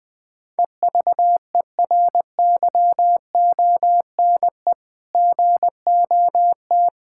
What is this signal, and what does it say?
Morse code, heard as a keyed tone, about 20 wpm, 700 Hz. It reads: EVERYONE GOT
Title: Sentences from Top 1000 Words - Review Part 5 - 20wpm